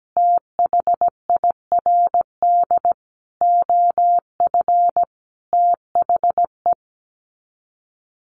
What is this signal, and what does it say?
Morse code, heard as a keyed tone, about 17 wpm, 700 Hz. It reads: THIRD OF THE
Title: Sets of 5 Words Encore2 17wpm